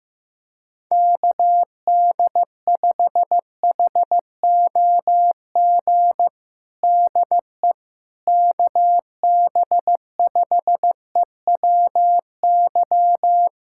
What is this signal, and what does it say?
Morse code, heard as a keyed tone, about 15 wpm, 700 Hz. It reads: KD5HOG DE KB5EWY
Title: USA Callsign DE Callsign - 6 Characters - 15wpm